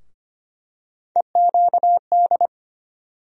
Morse code, heard as a keyed tone, about 25 wpm, 700 Hz. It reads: EQD
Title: USA Call Sign Postfixes - 3 Letters 25wpm